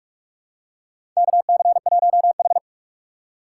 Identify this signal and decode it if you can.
Morse code, heard as a keyed tone, about 46 wpm, 690 Hz. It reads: KC1H
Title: USA Call Signs - 4 Characters - 3x Character Spacing - 45wpm